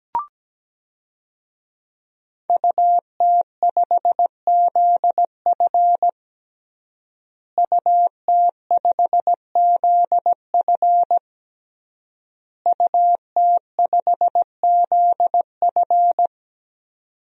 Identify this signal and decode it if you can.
Morse code, heard as a keyed tone, about 17 wpm, 700 Hz. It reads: UT5ZF UT5ZF UT5ZF
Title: DXCC-100 Call Signs - 5 Characters - Repeated 3x - 17wpm